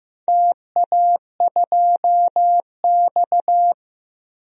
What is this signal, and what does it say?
Morse code, heard as a keyed tone, about 15 wpm, 700 Hz. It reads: TA2X
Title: DXCC-100 Call Signs - 4 Characters - Repeated 3x - 15wpm